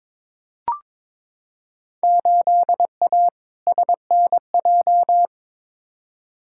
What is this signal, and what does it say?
Morse code, heard as a keyed tone, about 22 wpm, 700 Hz. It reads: 8A SNJ
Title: Field Day Exchanges Encore - 22wpm